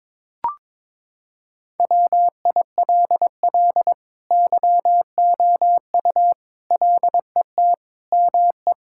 Morse code, heard as a keyed tone, about 22 wpm, 700 Hz. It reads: WILL YOU LET ME
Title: Sentences from Top 1600 Words 22wpm